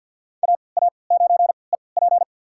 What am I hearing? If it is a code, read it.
Morse code, heard as a keyed tone, about 52 wpm, spaced as for 29 wpm, 700 Hz. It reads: AA9EP
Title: USA Call Signs - 5 Characters - 3x Character Spacing - 50wpm